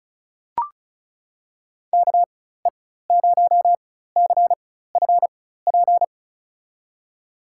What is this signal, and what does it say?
Morse code, heard as a keyed tone, about 36 wpm, spaced as for 16 wpm, 700 Hz. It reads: KE0CFP
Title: USA Call Signs - 6 Characters - 4x Character Spacing - 35wpm